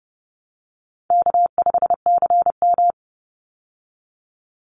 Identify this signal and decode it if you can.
Morse code, heard as a keyed tone, about 30 wpm, 700 Hz. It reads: K5CM
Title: Call Signs Repeated 2x - 30wpm